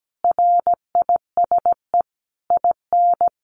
Morse code, heard as a keyed tone, about 17 wpm, 700 Hz. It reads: RISE IN
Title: A Sound Of Thunder 17wpm